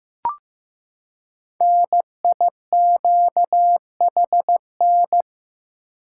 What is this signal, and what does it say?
Morse code, heard as a keyed tone, about 15 wpm, 700 Hz. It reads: NIQHN
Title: Code Groups - 5 Letter:Numbers - 15wpm